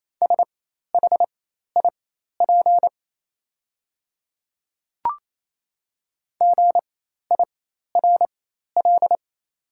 Morse code, heard as a keyed tone, about 27 wpm, spaced as for 13 wpm, 690 Hz. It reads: SHIP GIRL
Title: 4x Character Spacing - Top 500 Words - 28wpm